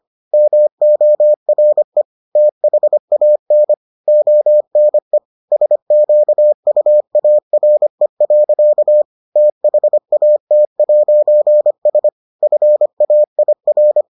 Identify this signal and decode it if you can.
Morse code, heard as a keyed tone, about 25 wpm, 600 Hz. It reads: MORE THAN ONE SQUARE. THAT'S FAIR